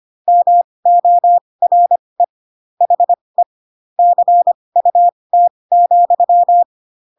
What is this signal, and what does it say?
Morse code, heard as a keyed tone, about 25 wpm, spaced as for 20 wpm, 700 Hz. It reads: MORE HE CUT,